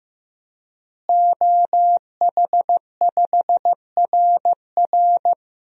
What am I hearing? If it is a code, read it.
Morse code, heard as a keyed tone, about 15 wpm, 700 Hz. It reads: OH5RR
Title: DXCC-100 Call Signs - 5 Characters - Repeated 3x - 15wpm